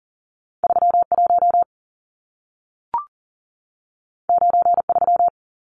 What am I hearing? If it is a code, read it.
Morse code, heard as a keyed tone, about 40 wpm, 700 Hz. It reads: 31 93
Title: Numbers - 2-Digit 40wpm